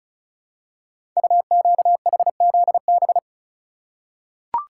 Morse code, heard as a keyed tone, about 36 wpm, 690 Hz. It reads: UQHZB
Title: Code Groups - 5 Letters - 35wpm